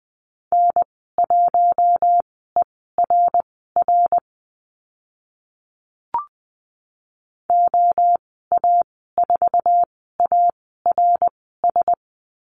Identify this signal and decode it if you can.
Morse code, heard as a keyed tone, about 20 wpm, spaced as for 14 wpm, 700 Hz. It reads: N1ERR OA4ARS
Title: 055 Lesson 10g - Letter r - Call Signs 20wpm